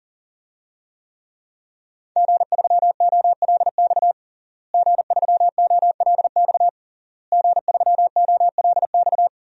Morse code, heard as a keyed tone, about 40 wpm, 700 Hz. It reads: G3OLX G3OLX G3OLX
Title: DXCC-100 Call Signs - 5 Characters - Repeated 3x - 40wpm